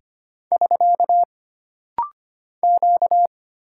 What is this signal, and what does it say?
Morse code, heard as a keyed tone, about 25 wpm, 700 Hz. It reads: <SK> Q
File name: Single Character (Letter:Number:Punctuation:Prosign) - Rapid-Fire - 25wpm